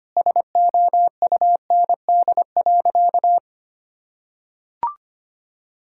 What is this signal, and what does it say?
Morse code, heard as a keyed tone, about 25 wpm, 700 Hz. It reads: SOUND.
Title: Sentences from Top 1000 Words - Review Part 1 - 25wpm